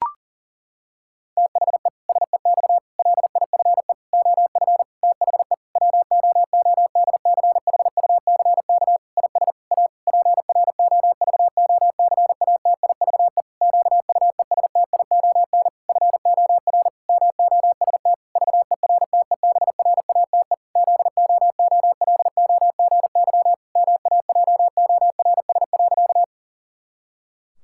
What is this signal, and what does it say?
Morse code, heard as a keyed tone, about 40 wpm, 700 Hz. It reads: THE SEX LIFE OF THE WOODCHUCK IS A PROVOCATIVE QUESTION FOR MOST VERTEBRATE ZOOLOGY MAJORS.
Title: Pangram Challenge v2 - 40wpm